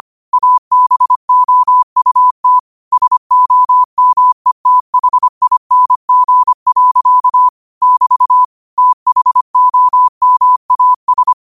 Morse code, heard as a keyed tone, about 25 wpm, 1000 Hz. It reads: ABOUT SOMETHING. <BT> THOMAS
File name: qotdfiles / 25wpm